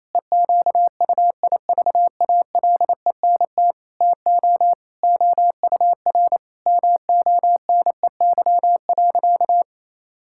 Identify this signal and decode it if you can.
Morse code, heard as a keyed tone, about 28 wpm, 700 Hz. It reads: EQUIVALENT TO OUR MONEY.
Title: Sentences from Top 1600 Words 28wpm